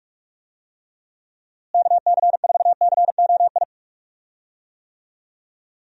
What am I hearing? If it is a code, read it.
Morse code, heard as a keyed tone, about 44 wpm, 700 Hz. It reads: KC4COI
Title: USA Call Signs - 6 Characters - Repeated 2x - 45wpm